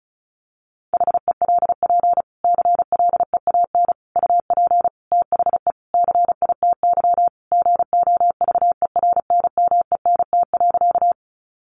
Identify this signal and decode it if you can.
Morse code, heard as a keyed tone, about 34 wpm, 700 Hz. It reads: HELP CLEAN UP THE CITY GOVERNMENT.
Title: Sentences from Top 1100 Words 35wpm